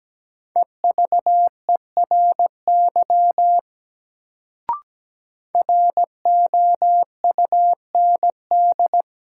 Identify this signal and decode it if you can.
Morse code, heard as a keyed tone, about 17 wpm, 700 Hz. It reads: EVERY ROUND
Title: Sentences from Top 200 Words - New Word Drill - 17wpm